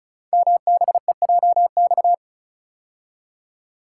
Morse code, heard as a keyed tone, about 36 wpm, 690 Hz. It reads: MBEJX
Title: Code Groups - 5 Letters - 35wpm